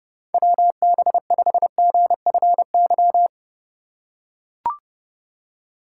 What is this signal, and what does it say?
Morse code, heard as a keyed tone, about 31 wpm, 700 Hz. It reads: WB5GFY
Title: USA Call Signs - 3x Character Spacing - 30wpm